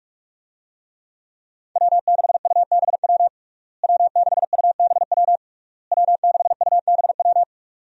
Morse code, heard as a keyed tone, about 46 wpm, 700 Hz. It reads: W6UBW W6UBW W6UBW
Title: USA Call Signs - 5 Characters - Repeated 3x - 45wpm